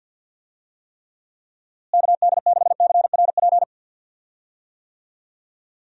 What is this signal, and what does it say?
Morse code, heard as a keyed tone, about 51 wpm, 700 Hz. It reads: KD6CRP
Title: USA Call Signs - 6 Characters - Repeated 3x - 50wpm